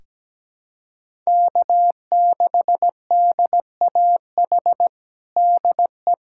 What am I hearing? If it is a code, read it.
Morse code, heard as a keyed tone, about 17 wpm, 700 Hz. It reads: K6DAH DE
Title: USA Callsign DE Callsign - 5 Characters - 17wpm